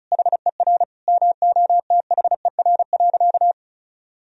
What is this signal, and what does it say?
Morse code, heard as a keyed tone, about 36 wpm, 700 Hz. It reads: HER MOTHER.
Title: Sentences from Top 1000 Words - Review Part 4 - 35wpm